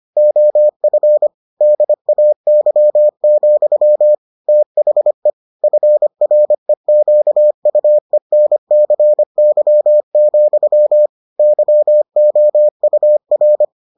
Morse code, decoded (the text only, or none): OF DAY, THE FREQUENCY, YOUR